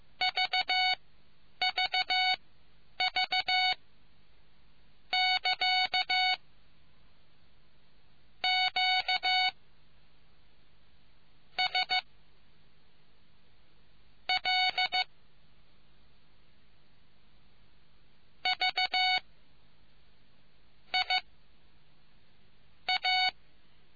Morse code, heard as a keyed tone, about 15 wpm, spaced as for 9 wpm, 740 Hz. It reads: VVV <KA> Q S L V I A